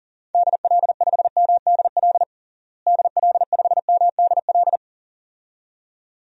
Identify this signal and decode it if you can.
Morse code, heard as a keyed tone, about 40 wpm, 700 Hz. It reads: DL5MDL DL5MDL
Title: DXCC-100 Call Signs - 6 Characters - Repeated 2x - 40wpm